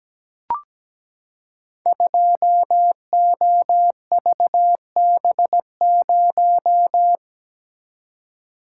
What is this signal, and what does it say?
Morse code, heard as a keyed tone, about 17 wpm, 700 Hz. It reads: 2OVB0
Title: Code Groups - 5 Letter:Numbers - 17wpm